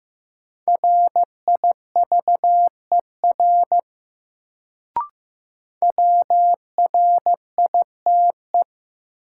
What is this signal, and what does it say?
Morse code, heard as a keyed tone, about 15 wpm, 700 Hz. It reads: RIVER WRITE
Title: Sentences from Top 300 Words - New Word Drill - 15wpm